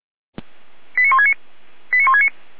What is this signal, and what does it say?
Five-tone selective call (ZVEI), heard as quick runs of five tones, tones of about 70 ms. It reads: 89169 89169